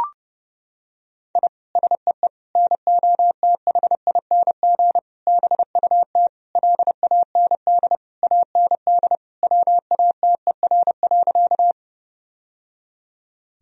Morse code, heard as a keyed tone, about 30 wpm, 700 Hz. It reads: I SEE NOTHING BUT LAND AND WATER.
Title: Sentences from Top 1000 Words - Review Part 1 - 30wpm